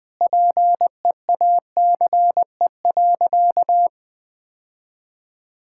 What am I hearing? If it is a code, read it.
Morse code, heard as a keyed tone, about 20 wpm, 700 Hz. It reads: PEACE.
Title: Sentences from Top 1600 Words 20wpm